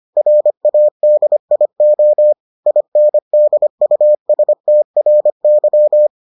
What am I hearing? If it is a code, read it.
Morse code, heard as a keyed tone, about 25 wpm, 600 Hz. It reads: RADIO INDUSTRY